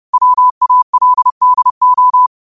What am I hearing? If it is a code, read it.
Morse code, heard as a keyed tone, about 30 wpm, 1000 Hz. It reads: WALDO
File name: 30wpm